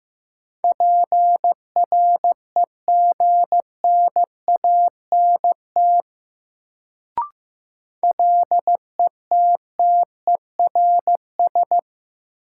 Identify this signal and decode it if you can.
Morse code, heard as a keyed tone, about 15 wpm, 700 Hz. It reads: PREGNANT LETTERS
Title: Sentences from Top 1000 Words - Review Part 6 - New Word Drill - 15wpm